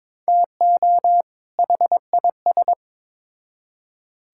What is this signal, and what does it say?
Morse code, heard as a keyed tone, about 21 wpm, 700 Hz. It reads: TO HIS
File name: Build A Phrase - 2-Word Phrases - 22wpm